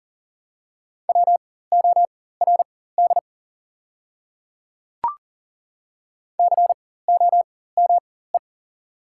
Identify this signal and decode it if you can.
Morse code, heard as a keyed tone, about 41 wpm, spaced as for 19 wpm, 700 Hz. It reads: WORD COME
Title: 4x Character Spacing - Top 100 Words - 40wpm